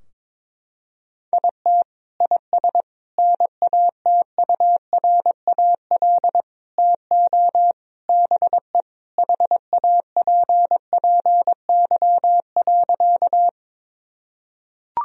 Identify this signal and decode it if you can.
Morse code, heard as a keyed tone, about 22 wpm, 700 Hz. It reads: IT IS NATURAL TO BE HAPPY.
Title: Sentences from Top 1000 Words - Review Part 4 - 22wpm